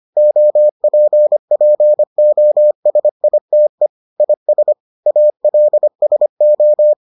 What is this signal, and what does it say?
Morse code, heard as a keyed tone, about 25 wpm, 600 Hz. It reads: OPPOSITE IS ALSO